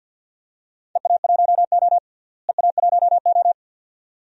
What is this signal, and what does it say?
Morse code, heard as a keyed tone, about 47 wpm, 700 Hz. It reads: EA1O EA1O
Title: DXCC-100 Call Signs - 4 Characters - Repeated 2x - 50wpm